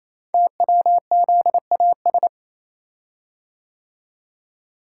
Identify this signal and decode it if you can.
Morse code, heard as a keyed tone, about 27 wpm, 700 Hz. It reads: TWZAS
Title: Code Groups - 5 Letters - 28wpm